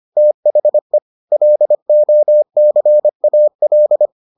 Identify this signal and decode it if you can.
Morse code, heard as a keyed tone, about 25 wpm, 600 Hz. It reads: THE LOCAL